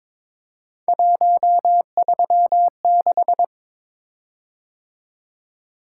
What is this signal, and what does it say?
Morse code, heard as a keyed tone, about 22 wpm, 700 Hz. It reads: 136
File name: practice / 22wpm / Three Letter-Number Combinations - 22wpm